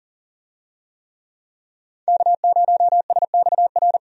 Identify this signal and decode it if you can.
Morse code, heard as a keyed tone, about 38 wpm, 700 Hz. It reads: K0SXR
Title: USA Call Signs - 5 Characters - 40wpm